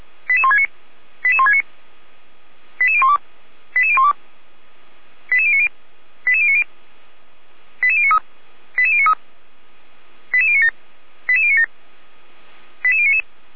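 Five-tone selective call (ZVEI), heard as quick runs of five tones, tones of about 70 ms. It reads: 88169 88169 88012 88012 88090 88090 88093 88093 88097 88097 88099
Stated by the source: ZVEI five-tone call 88074 received by SDR.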